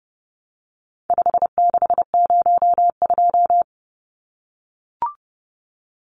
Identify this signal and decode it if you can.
Morse code, heard as a keyed tone, about 30 wpm, 700 Hz. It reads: 5602
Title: Numbers - 4-Digit 30wpm